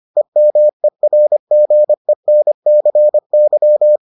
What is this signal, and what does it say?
Morse code, heard as a keyed tone, about 25 wpm, 600 Hz. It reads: EMERGENCY